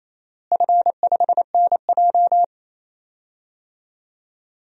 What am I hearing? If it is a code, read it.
Morse code, heard as a keyed tone, about 28 wpm, 700 Hz. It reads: F5NJ
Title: DXCC-100 Call Signs - 4 Characters - Repeated 3x - 28wpm